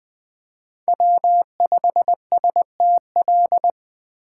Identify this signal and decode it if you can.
Morse code, heard as a keyed tone, about 20 wpm, 700 Hz. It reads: W5STL
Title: USA Call Signs - 5 Characters - 8x Character Spacing - 20wpm